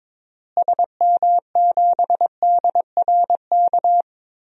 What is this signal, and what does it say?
Morse code, heard as a keyed tone, about 22 wpm, 700 Hz. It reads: SM7DRK
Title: DXCC-100 Call Signs - 6 Characters - Repeated 2x - 22wpm